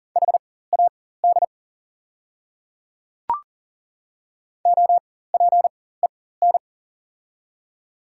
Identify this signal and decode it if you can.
Morse code, heard as a keyed tone, about 40 wpm, spaced as for 19 wpm, 700 Hz. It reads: HAD OPEN
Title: 4x Character Spacing - Top 300 Words - 40wpm